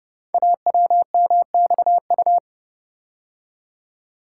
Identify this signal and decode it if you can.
Morse code, heard as a keyed tone, about 30 wpm, 700 Hz. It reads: AWMXU
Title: Code Groups - 5 Letters - 30wpm